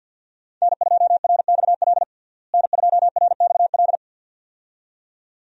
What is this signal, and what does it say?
Morse code, heard as a keyed tone, about 51 wpm, 700 Hz. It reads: N2RXL N2RXL
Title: USA Call Signs - 5 Characters - Repeated 2x - 50wpm